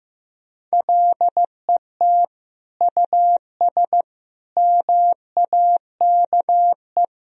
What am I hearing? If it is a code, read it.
Morse code, heard as a keyed tone, about 15 wpm, 700 Hz. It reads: LET US MAKE
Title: Sentences from Top 1000 Words - Review Part 4 - 15wpm